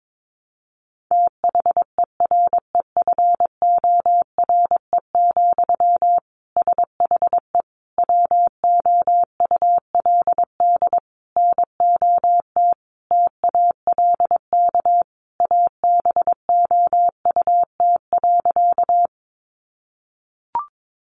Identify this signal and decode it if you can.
Morse code, heard as a keyed tone, about 22 wpm, 700 Hz. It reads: THEREFORE, SHE WOULD NOT TALK ABOUT.
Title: Sentences from Top 800 Words 22wpm